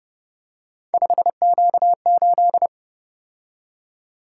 Morse code, heard as a keyed tone, about 29 wpm, 700 Hz. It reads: 5Q8
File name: Speed-Racing - Three Letter-Number Combinations - 25wpm